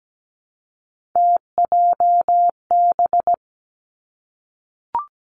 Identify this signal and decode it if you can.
Morse code, heard as a keyed tone, about 17 wpm, 700 Hz. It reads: TJB
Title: Three Challenging Letter-Number Combinations 17wpm